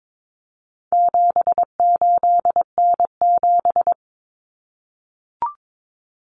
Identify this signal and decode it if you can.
Morse code, heard as a keyed tone, about 22 wpm, 700 Hz. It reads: 78N7
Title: Common Cut Numbers - 4-Digit 22wpm